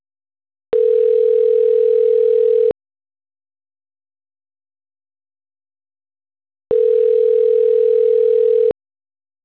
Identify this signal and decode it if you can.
Ringback tone (North America), 440 + 480 Hz.